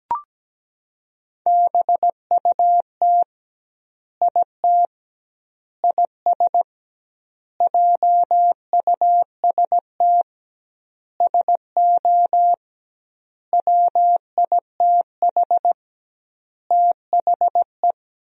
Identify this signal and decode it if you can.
Morse code, heard as a keyed tone, about 17 wpm, 700 Hz. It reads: BUT IT IS JUST SO WITH THE
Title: 2x Word-Spacing - Sentences from Top 200 Words - 17wpm